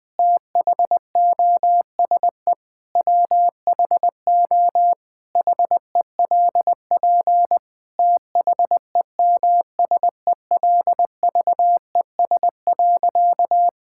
Morse code, heard as a keyed tone, about 20 wpm, 700 Hz. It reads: THOSE WHO HELP THEMSELVES.
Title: Sentences from Top 1000 Words - Review Part 5 - 20wpm